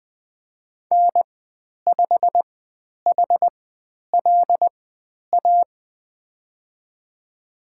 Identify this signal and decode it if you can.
Morse code, heard as a keyed tone, about 20 wpm, spaced as for 10 wpm, 700 Hz. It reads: N5HLA
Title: 095 Lesson 16g - Letter c - Call Signs 20wpm